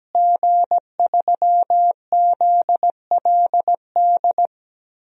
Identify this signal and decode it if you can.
Morse code, heard as a keyed tone, about 17 wpm, 700 Hz. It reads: G3ZLD
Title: DXCC-100 Call Signs - 5 Characters - Repeated 3x - 17wpm